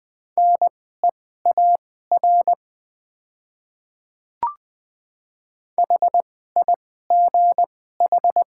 Morse code, heard as a keyed tone, about 20 wpm, spaced as for 15 wpm, 700 Hz. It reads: NEAR HIGH
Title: 2x Character Spacing - Top 300 Words - 20wpm